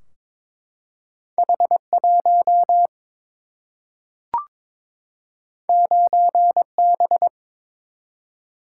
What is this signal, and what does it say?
Morse code, heard as a keyed tone, about 22 wpm, 700 Hz. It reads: H 1 9 B
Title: Single-Letter Single-Number 22wpm